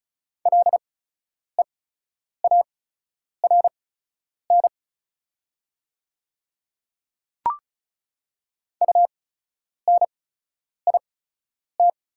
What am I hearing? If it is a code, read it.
Morse code, heard as a keyed tone, about 36 wpm, spaced as for 10 wpm, 700 Hz. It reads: LEARN UNIT
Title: 8x Character Spacing - Top 500 Words - 35wpm